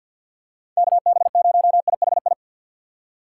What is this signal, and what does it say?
Morse code, heard as a keyed tone, about 47 wpm, 700 Hz. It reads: KB0IHI
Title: USA Call Signs - 6 Characters - 3x Character Spacing - 50wpm